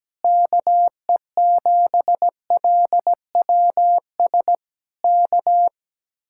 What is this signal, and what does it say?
Morse code, heard as a keyed tone, about 17 wpm, 700 Hz. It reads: KE7LWS K
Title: Speed-Racing Common QSO Phrases - 17wpm